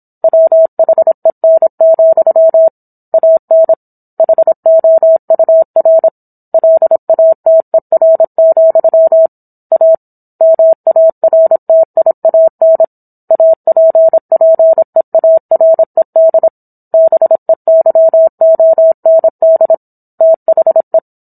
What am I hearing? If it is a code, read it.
Morse code, heard as a keyed tone, about 27 wpm, 650 Hz. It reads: WHEN, AN HOUR LATER, A MARTIAN APPEARED BEYOND THE